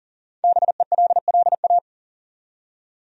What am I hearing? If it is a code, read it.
Morse code, heard as a keyed tone, about 40 wpm, 700 Hz. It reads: BELLA
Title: USA Common Names - 40wpm